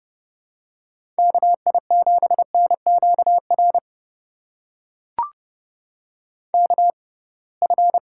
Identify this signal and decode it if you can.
Morse code, heard as a keyed tone, about 32 wpm, spaced as for 23 wpm, 700 Hz. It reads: KI7NQR KF
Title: USA Call Signs - 6 Characters - 6x Character Spacing - 30wpm